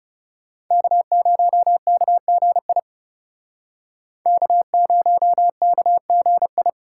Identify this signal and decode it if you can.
Morse code, heard as a keyed tone, about 38 wpm, 700 Hz. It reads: K0KGI K0KGI